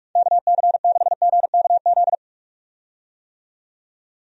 Morse code, heard as a keyed tone, about 44 wpm, 700 Hz. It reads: KC6GKZ
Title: USA Call Signs - 6 Characters - 45wpm